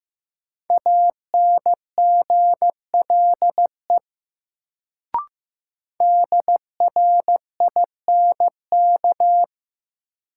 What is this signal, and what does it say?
Morse code, heard as a keyed tone, about 15 wpm, 700 Hz. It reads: ANGLE DRINK
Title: Sentences from Top 1500 Words - New Word Drill - 15wpm